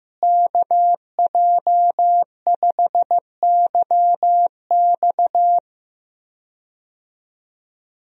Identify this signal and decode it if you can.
Morse code, heard as a keyed tone, about 15 wpm, 700 Hz. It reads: KJ5YX
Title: USA Call Signs - 5 Characters - Repeated 2x - 15wpm